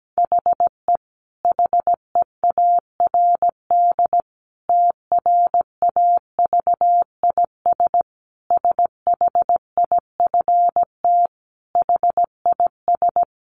Morse code, heard as a keyed tone, about 17 wpm, 700 Hz. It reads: HE HEARD TRAVIS SHIFT HIS
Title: A Sound Of Thunder 17wpm